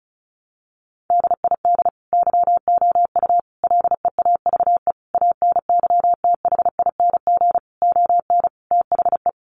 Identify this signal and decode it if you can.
Morse code, heard as a keyed tone, about 34 wpm, 700 Hz. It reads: DID YOU LEAVE ANYTHING ON THE
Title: Sentences from Top 1000 Words 35wpm